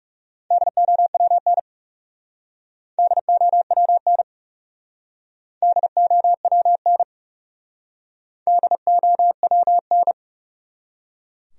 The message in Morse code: DOWN DOWN DOWN DOWN

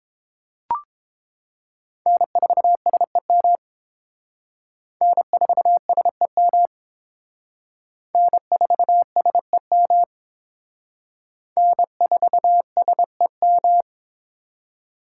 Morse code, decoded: N4SEM N4SEM N4SEM N4SEM

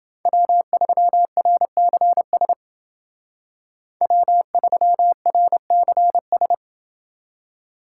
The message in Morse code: W3RCS W3RCS

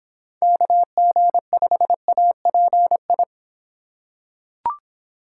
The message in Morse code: KG5API

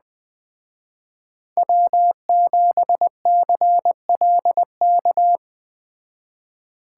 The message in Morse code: W7CLK

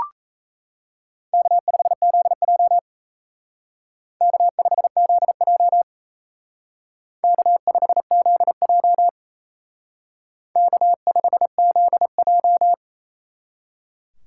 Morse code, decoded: K5ZJ K5ZJ K5ZJ K5ZJ

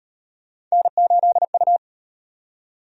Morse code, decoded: N8U

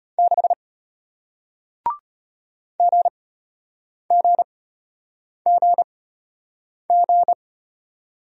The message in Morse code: 6 GGGG

